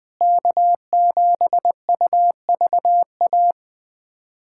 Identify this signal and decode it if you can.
Morse code, heard as a keyed tone, about 20 wpm, 700 Hz. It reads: K7UVA
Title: USA Callsign DE Callsign - 5 Characters - 20wpm